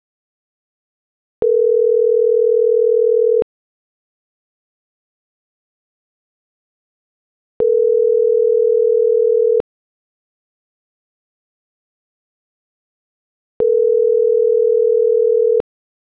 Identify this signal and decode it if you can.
Ringback tone (North America), 440 + 480 Hz.
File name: CENTRAL MORENA / out murillooscar.general